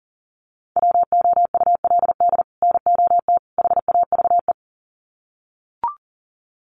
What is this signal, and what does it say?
Morse code, heard as a keyed tone, about 40 wpm, 700 Hz. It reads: WOULD NOT HAVE
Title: Sets of 3 Words 40wpm